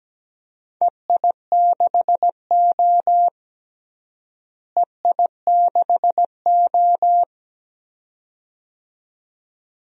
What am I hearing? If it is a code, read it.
Morse code, heard as a keyed tone, about 17 wpm, 700 Hz. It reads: EI6O EI6O
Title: DXCC-100 Call Signs - 4 Characters - Repeated 2x - 17wpm